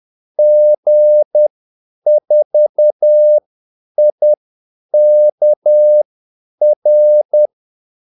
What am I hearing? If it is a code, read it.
Morse code, heard as a keyed tone, about 10 wpm, spaced as for 8 wpm, 600 Hz. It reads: G4IKR
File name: Callsigns-10-8wpm